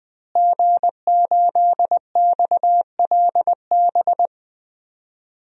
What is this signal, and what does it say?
Morse code, heard as a keyed tone, about 20 wpm, 700 Hz. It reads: G8XLB
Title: DXCC-100 Call Signs - 5 Characters - Repeated 3x - 20wpm